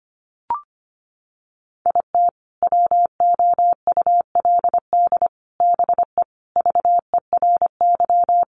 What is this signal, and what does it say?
Morse code, heard as a keyed tone, about 25 wpm, 700 Hz. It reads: IT WOULD BE VERY
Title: Sentences from Top 200 Words 25wpm